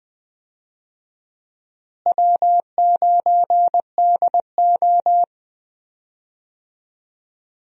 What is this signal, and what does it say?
Morse code, heard as a keyed tone, about 20 wpm, 700 Hz. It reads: W9DO
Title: USA Call Signs - 4 Characters - Repeated 3x - 20wpm